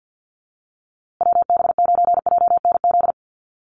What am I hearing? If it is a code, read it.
Morse code, heard as a keyed tone, about 50 wpm, 700 Hz. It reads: WB9JNZ
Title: USA Call Signs Repeated 2x 50wpm